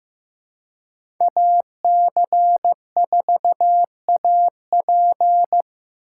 Morse code, heard as a keyed tone, about 15 wpm, 700 Hz. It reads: AC4AP
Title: USA Call Signs - 5 Characters - 3x Character Spacing - 15wpm